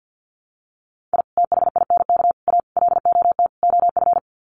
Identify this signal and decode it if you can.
Morse code, heard as a keyed tone, about 50 wpm, 700 Hz. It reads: I THINK A LOT OF
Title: Sets of 5 Words 50wpm